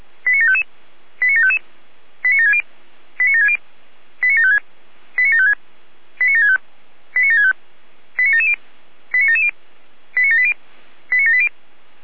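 Five-tone selective call (ZVEI), heard as quick runs of five tones, tones of about 80 ms. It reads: 89755 89755 89760 89760 89756 89756 89765 89765 89770 89770 89780 89780
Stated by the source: ZVEI five-tone selective call 89705 received with an SDR.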